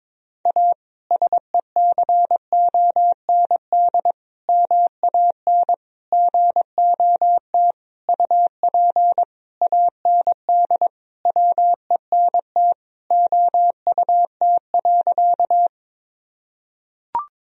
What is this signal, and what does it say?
Morse code, heard as a keyed tone, about 22 wpm, 700 Hz. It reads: A SECOND MAN GOT UP AND WENT OUT.
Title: Sentences from Top 1000 Words - Review Part 1 - 22wpm